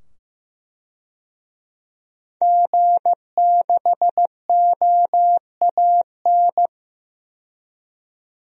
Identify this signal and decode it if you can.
Morse code, heard as a keyed tone, about 15 wpm, 700 Hz. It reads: G6OAN
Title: DXCC-100 Call Signs - 5 Characters - 15wpm